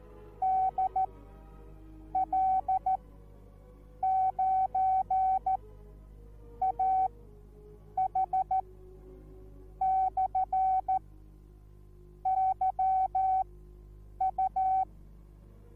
Morse code, heard as a keyed tone, about 14 wpm, spaced as for 6 wpm, 760 Hz. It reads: DL9AH/YU